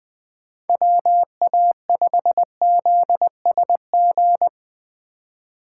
Morse code, heard as a keyed tone, about 20 wpm, 700 Hz. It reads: WA5ZSG
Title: USA Call Signs - 3x Character Spacing - 20wpm